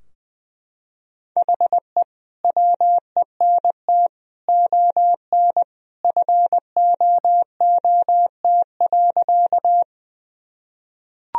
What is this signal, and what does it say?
Morse code, heard as a keyed tone, about 21 wpm, 700 Hz. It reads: HE WENT ON FOOT.
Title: Sentences from Top 1000 Words - Review Part 3 - 20wpm